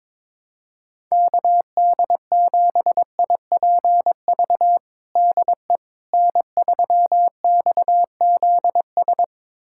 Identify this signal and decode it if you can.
Morse code, heard as a keyed tone, about 22 wpm, 700 Hz. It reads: KD7IPV DE N3XZS
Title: USA Callsign DE Callsign 22wpm